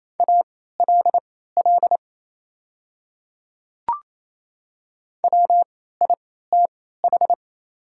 Morse code, heard as a keyed tone, about 28 wpm, spaced as for 16 wpm, 690 Hz. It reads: ALL WITH
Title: 3x Character Spacing - Top 100 Words - 28wpm